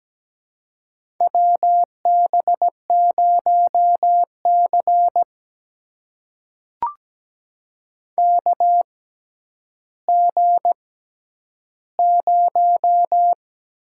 Morse code, heard as a keyed tone, about 17 wpm, spaced as for 10 wpm, 700 Hz. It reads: WB0C KG0
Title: USA Call Signs - 6x Character Spacing - 17wpm